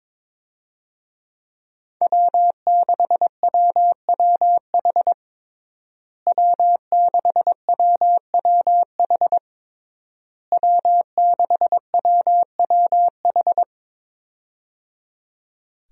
Morse code, heard as a keyed tone, about 22 wpm, 700 Hz. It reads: W6WWH W6WWH W6WWH
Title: USA Call Signs - 5 Characters - Repeated 3x - 22wpm